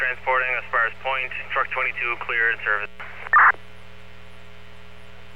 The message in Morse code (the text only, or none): none